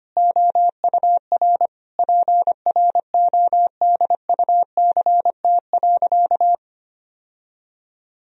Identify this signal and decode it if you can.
Morse code, heard as a keyed tone, about 25 wpm, 700 Hz. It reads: OUR PRODUCT.
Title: Sentences from Top 1000 Words - Review Part 3 - 25wpm